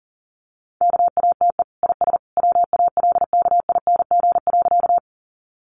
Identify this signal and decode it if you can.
Morse code, heard as a keyed tone, about 40 wpm, 700 Hz. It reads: KATE IS WALKING.
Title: Easy Phrases 40wpm